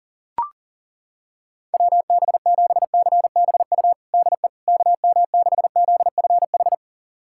Morse code, heard as a keyed tone, about 40 wpm, 700 Hz. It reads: WB7CBU DE KM6ZFH
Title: USA Callsign DE Callsign - 6 Characters - 40wpm